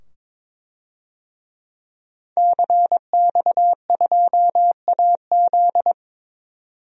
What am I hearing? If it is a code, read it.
Morse code, heard as a keyed tone, about 21 wpm, 700 Hz. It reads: CX2AZ
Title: DXCC-100 Call Signs - 5 Characters - 22wpm